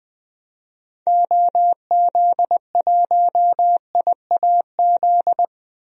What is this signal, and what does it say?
Morse code, heard as a keyed tone, about 21 wpm, 700 Hz. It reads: OZ1IAZ
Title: DXCC-100 Call Signs - 6 Characters - Repeated 2x - 20wpm